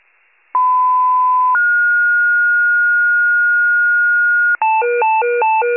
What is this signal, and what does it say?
Two-tone page, heard as a one-second tone, then a three-second tone, about 1010 then 1430 Hz. It is heard from 0.5 to 4.6 s.
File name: wcalert